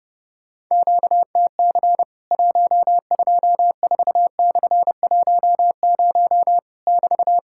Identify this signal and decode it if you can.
Morse code, heard as a keyed tone, about 30 wpm, 700 Hz. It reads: QTC 124/10 <BT>
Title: Worked All Europe DX Contest - QTC Exchange - 30wpm